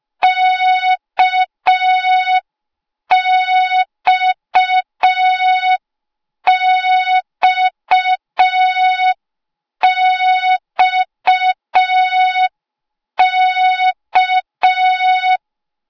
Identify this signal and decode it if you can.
Morse code, heard as a keyed tone, about 5 wpm, 3700 Hz. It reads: KXXXK
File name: Morze-HA2MN / hang / 08KXC HA2MN